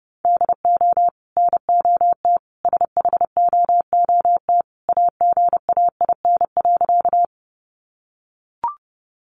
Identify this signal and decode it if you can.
Morse code, heard as a keyed tone, about 30 wpm, 700 Hz. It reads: DO NOT SHOOT AGAIN.
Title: Sentences from Top 1200 Words 30wpm